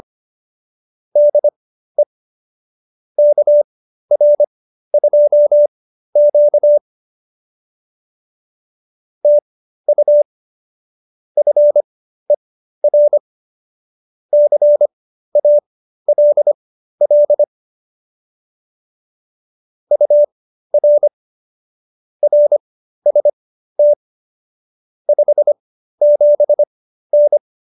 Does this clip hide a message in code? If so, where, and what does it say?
Yes: Morse code from 1.2 to 27.4 s, heard as a keyed tone, about 25 wpm, spaced as for 13 wpm, 590 Hz. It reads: DE KR2Q TU FER CALL UR RST 57N